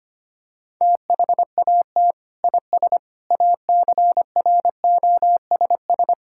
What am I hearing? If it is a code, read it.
Morse code, heard as a keyed tone, about 25 wpm, 700 Hz. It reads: THAT IS ACROSS
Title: Sentences from Top 1000 Words - Review Part 5 - 25wpm